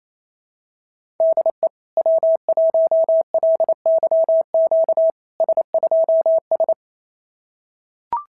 Morse code, heard as a keyed tone, about 28 wpm, 650 Hz. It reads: DE W1LYQ S2S
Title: SOTA QSO Practice 28wpm